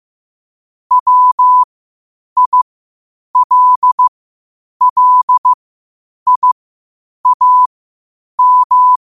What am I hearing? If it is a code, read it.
Morse code, heard as a keyed tone, about 15 wpm, spaced as for 8 wpm, 1000 Hz. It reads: WILLIAM
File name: qotdfiles / 8wpm